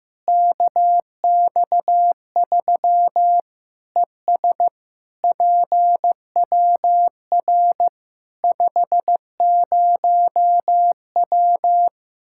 Morse code, heard as a keyed tone, about 15 wpm, 700 Hz. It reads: KX3 ES PWR 50W
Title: Practice QSO - Exchange 2 - Info - 15wpm